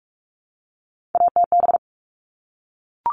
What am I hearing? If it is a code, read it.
Morse code, heard as a keyed tone, about 44 wpm, 700 Hz. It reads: ATB
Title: All Cut Numbers - 3-Digit 45wpm